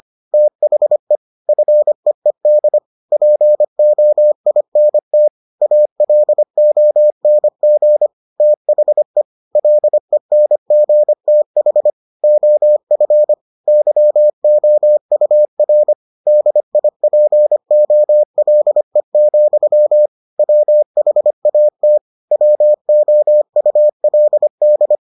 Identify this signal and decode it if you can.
Morse code, heard as a keyed tone, about 25 wpm, 600 Hz. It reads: THE FEED POINT ALONG THE LENGTH OF YOUR DIPOLE, WHAT WOULD